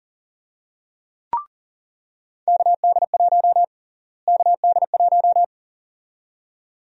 Morse code, heard as a keyed tone, about 37 wpm, 700 Hz. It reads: KD1 KD1
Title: USA Call Sign Prefixes - Repeated 2x - 40wpm